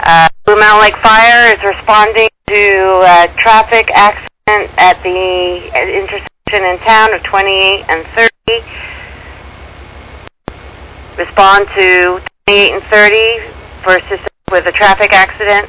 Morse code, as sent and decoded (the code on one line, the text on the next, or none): none